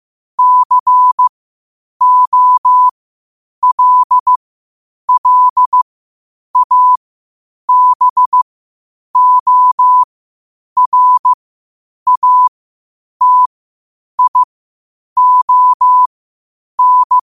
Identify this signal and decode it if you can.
Morse code, heard as a keyed tone, about 15 wpm, spaced as for 8 wpm, 1000 Hz. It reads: COLLABORATION
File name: qotdfiles / 8wpm